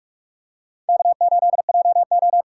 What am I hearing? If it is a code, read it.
Morse code, heard as a keyed tone, about 43 wpm, 700 Hz. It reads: K8JO